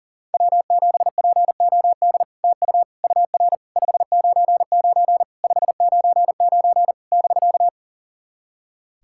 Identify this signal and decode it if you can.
Morse code, heard as a keyed tone, about 37 wpm, 700 Hz. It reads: W7POD TU UR 599 599 <BK>
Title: SOTA QSO Practice 40wpm